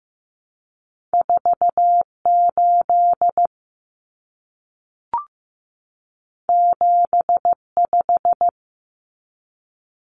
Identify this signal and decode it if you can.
Morse code, heard as a keyed tone, about 15 wpm, 700 Hz. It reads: 48 75
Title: Common Cut Numbers - 2-Digit 15wpm